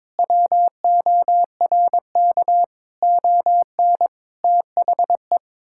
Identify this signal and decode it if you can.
Morse code, heard as a keyed tone, about 22 wpm, 700 Hz. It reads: WORK ON THE
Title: Sentences from Top 1000 Words - Review Part 5 - 22wpm